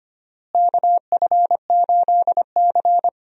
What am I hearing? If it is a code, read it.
Morse code, heard as a keyed tone, about 25 wpm, 700 Hz. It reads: KF8C
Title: USA Call Signs - 4 Characters - Repeated 2x - 25wpm